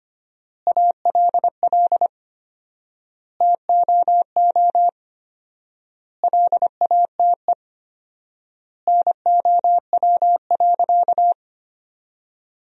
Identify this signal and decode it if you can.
Morse code, heard as a keyed tone, about 25 wpm, 700 Hz. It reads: ALL TOO LATE NOW.
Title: 4x Word-Spacing - Sentences from Top 300 Words - 25wpm